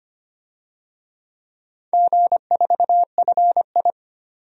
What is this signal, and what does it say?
Morse code, heard as a keyed tone, about 25 wpm, 700 Hz. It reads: G4FI
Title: DXCC-100 Call Signs - 4 Characters - 25wpm